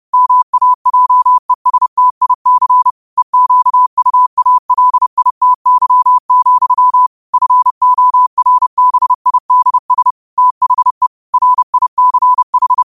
Morse code, heard as a keyed tone, about 30 wpm, 1000 Hz. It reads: MAJESTIC EQUALITY, FORBIDS THE RICH
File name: qotdfiles / 30wpm